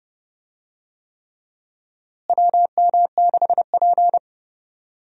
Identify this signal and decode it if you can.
Morse code, heard as a keyed tone, about 30 wpm, 700 Hz. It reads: WM6P
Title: USA Call Signs - 4 Characters - Repeated 3x - 30wpm